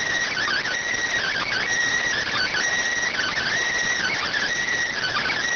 Emergency Alert System (SAME) tones, heard as harsh buzzing bursts. It is heard from 1.2 to 5.4 s.